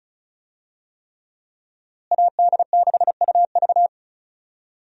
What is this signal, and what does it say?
Morse code, heard as a keyed tone, about 36 wpm, 700 Hz. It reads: AD6UV
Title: USA Call Signs - 5 Characters - Repeated 3x - 35wpm